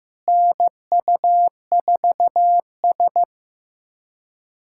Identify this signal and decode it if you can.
Morse code, heard as a keyed tone, about 15 wpm, 700 Hz. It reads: NU4S
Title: USA Callsign DE Callsign - 4 Characters - 15wpm